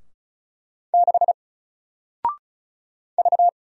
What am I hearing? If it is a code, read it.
Morse code, heard as a keyed tone, about 36 wpm, 700 Hz. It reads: 6 V
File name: Single-Letter-Number-35wpm